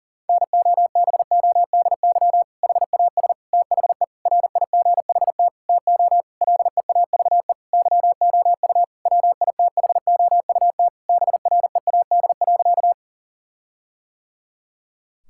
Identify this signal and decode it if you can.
Morse code, heard as a keyed tone, about 40 wpm, 700 Hz. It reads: NOBODY HAS THE RIGHT TO LEAVE YOU WITHOUT BREAD.
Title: Sentences from Top 1600 Words 40wpm